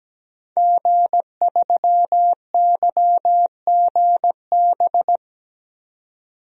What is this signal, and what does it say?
Morse code, heard as a keyed tone, about 17 wpm, 700 Hz. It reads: G3YGB
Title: DXCC-100 Call Signs - 5 Characters - 17wpm